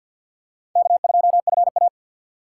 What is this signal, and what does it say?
Morse code, heard as a keyed tone, about 53 wpm, 700 Hz. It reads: K2FA
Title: USA Call Signs - 4 Characters - 6x Character Spacing - 50wpm